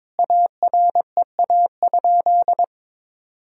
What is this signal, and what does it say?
Morse code, heard as a keyed tone, about 22 wpm, 700 Hz. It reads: AREA?
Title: Sentences from Top 1000 Words - Review Part 2 - 22wpm